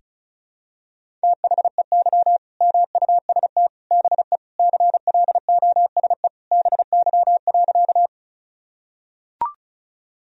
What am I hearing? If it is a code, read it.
Morse code, heard as a keyed tone, about 36 wpm, 700 Hz. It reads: THEY MUST BE CLOSE BY.
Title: Sentences from Top 1000 Words - Review Part 1 - 35wpm